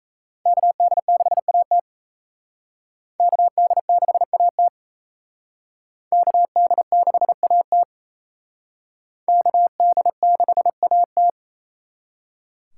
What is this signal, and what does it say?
Morse code, heard as a keyed tone, about 34 wpm, 700 Hz. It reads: KD6AT KD6AT KD6AT KD6AT
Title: Speed Racing - USA 2x2 Call Signs 28wpm